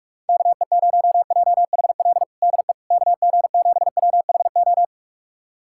Morse code, heard as a keyed tone, about 46 wpm, 700 Hz. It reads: KE0JHL DE KG7WHO
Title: USA Callsign DE Callsign 45wpm